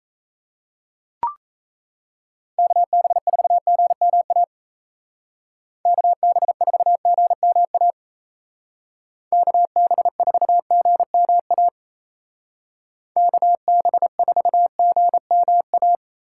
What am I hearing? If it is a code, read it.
Morse code, heard as a keyed tone, about 36 wpm, 700 Hz. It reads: KB4GMA KB4GMA KB4GMA KB4GMA